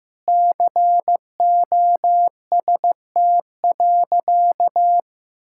Morse code, heard as a keyed tone, about 15 wpm, 700 Hz. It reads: COST.
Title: Sentences from Top 1000 Words - Review Part 6 - 15wpm